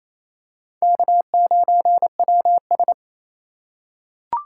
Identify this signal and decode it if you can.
Morse code, heard as a keyed tone, about 28 wpm, 700 Hz. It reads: K9WS
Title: USA Call Signs - 2x Character Spacing - 28wpm